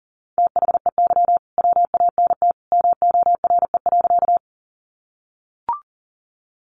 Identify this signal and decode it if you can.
Morse code, heard as a keyed tone, about 40 wpm, 700 Hz. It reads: THEY WANT MORE.
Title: Sentences from Top 200 Words 40wpm